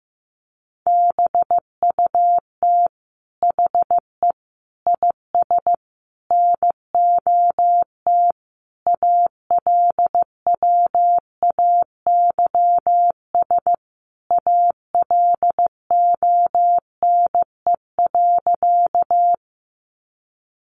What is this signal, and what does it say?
Morse code, heard as a keyed tone, about 15 wpm, 700 Hz. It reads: BUT HE IS NOT ALWAYS ALONE.
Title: Sentences from Top 1000 Words 15wpm